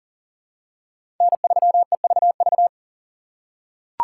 Morse code, heard as a keyed tone, about 42 wpm, 690 Hz. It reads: N3EVV
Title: USA Call Signs - 5 Characters - 3x Character Spacing - 40wpm